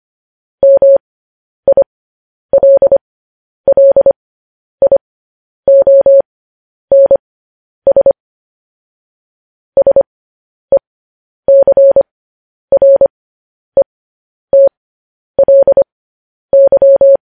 Morse code, heard as a keyed tone, about 25 wpm, spaced as for 10 wpm, 550 Hz. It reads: MILLIONS SECRETLY
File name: podcast / News headlines-1x-25x10